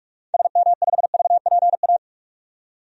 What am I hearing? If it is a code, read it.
Morse code, heard as a keyed tone, about 45 wpm, 700 Hz. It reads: SM5VPA
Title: DXCC-100 Call Signs - 6 Characters - 45wpm